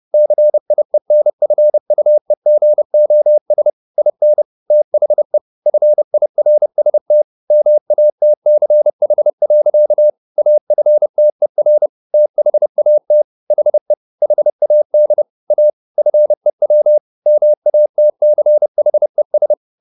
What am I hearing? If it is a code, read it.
Morse code, heard as a keyed tone, about 30 wpm, 600 Hz. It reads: CIENFUEGOS IN THE FIRST MATCH. AFTER THAT HE HAD A FEW MATCHES